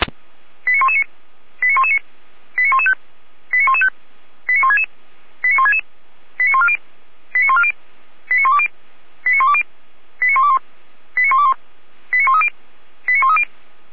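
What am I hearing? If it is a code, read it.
Five-tone selective call (ZVEI), heard as quick runs of five tones, tones of about 70 ms. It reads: 89119 89119 89115 89115 89166 89166 89140 89140 89120 89120 89121 89121 89130 89130
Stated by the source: ZVEI five-tone selective call received by SDR (455).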